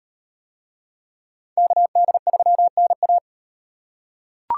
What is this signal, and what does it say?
Morse code, heard as a keyed tone, about 34 wpm, 700 Hz. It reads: KD3NA